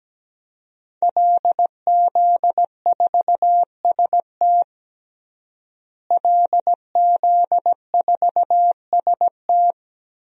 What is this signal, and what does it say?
Morse code, heard as a keyed tone, about 17 wpm, 700 Hz. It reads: LZ4ST LZ4ST
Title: DXCC-100 Call Signs - 5 Characters - Repeated 2x - 17wpm